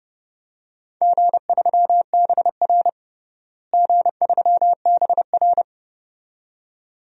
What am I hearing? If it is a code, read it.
Morse code, heard as a keyed tone, about 32 wpm, 700 Hz. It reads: G3BR G3BR
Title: DXCC-100 Call Signs - 4 Characters - Repeated 2x - 30wpm